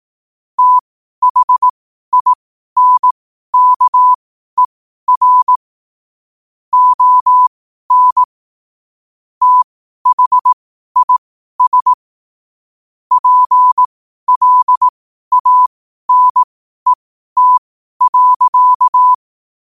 Morse code, heard as a keyed tone, about 18 wpm, spaced as for 12 wpm, 1000 Hz. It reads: THINKER ON THIS PLANET.
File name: 12wpm